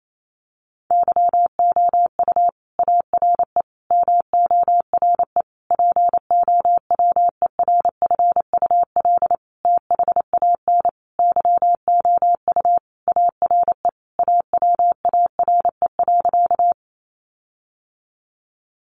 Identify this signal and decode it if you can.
Morse code, heard as a keyed tone, about 28 wpm, 700 Hz. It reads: YOU ARE MORE POWERFUL THAN YOU ARE AWARE.
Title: Sentences from Top 1300 Words 28wpm